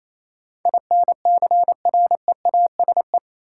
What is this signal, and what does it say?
Morse code, heard as a keyed tone, about 28 wpm, 700 Hz. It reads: INCREASE
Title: Sets of 5 Words - New Word Drill - 28wpm